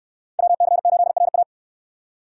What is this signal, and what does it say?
Morse code, heard as a keyed tone, about 68 wpm, 690 Hz. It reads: WC8RA